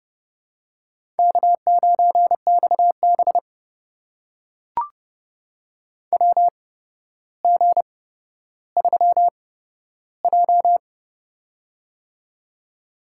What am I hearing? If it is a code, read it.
Morse code, heard as a keyed tone, about 30 wpm, spaced as for 13 wpm, 700 Hz. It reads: K9XB WG3J
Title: USA Call Signs - 4 Characters - 8x Character Spacing - 30wpm